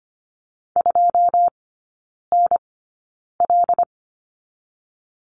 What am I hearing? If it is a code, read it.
Morse code, heard as a keyed tone, about 25 wpm, spaced as for 9 wpm, 690 Hz. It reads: 2NL
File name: Lesson 14e - Number 2 - Three Characters 25wpm